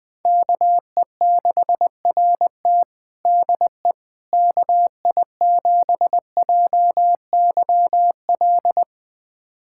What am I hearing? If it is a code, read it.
Morse code, heard as a keyed tone, about 20 wpm, 700 Hz. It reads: KE6RT DE KI7JYL
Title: USA Callsign DE Callsign 20wpm